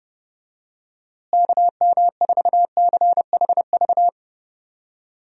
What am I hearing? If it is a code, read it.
Morse code, heard as a keyed tone, about 30 wpm, 700 Hz. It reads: KM4CHV
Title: USA Call Signs - 6 Characters - Repeated 2x - 30wpm